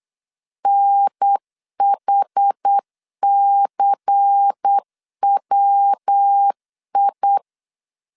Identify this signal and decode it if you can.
Morse code, heard as a keyed tone, about 8 wpm, 800 Hz. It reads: NHCWI
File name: 8wpm